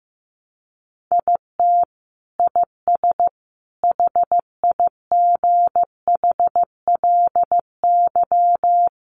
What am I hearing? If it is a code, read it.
Morse code, heard as a keyed tone, about 15 wpm, 700 Hz. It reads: IT IS HIGHLY
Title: Sentences from Top 1300 Words 15wpm